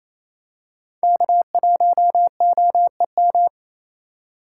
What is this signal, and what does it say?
Morse code, heard as a keyed tone, about 28 wpm, 700 Hz. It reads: K1OEM
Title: USA Call Signs - 4x Character Spacing - 28wpm